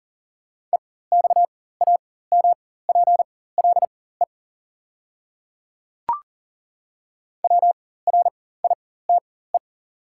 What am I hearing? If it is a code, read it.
Morse code, heard as a keyed tone, about 39 wpm, spaced as for 19 wpm, 690 Hz. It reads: EXAMPLE WRITE
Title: 4x Character Spacing - Top 300 Words - 40wpm